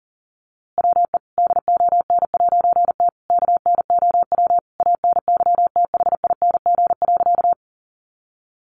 Morse code, heard as a keyed tone, about 40 wpm, 700 Hz. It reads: WE DON'T KNOW ANYTHING.
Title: A Sound Of Thunder 40wpm